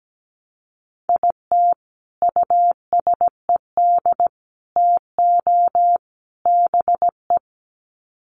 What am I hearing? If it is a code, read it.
Morse code, heard as a keyed tone, about 17 wpm, 700 Hz. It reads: IT USED TO BE
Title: Sets of 4 Words 17wpm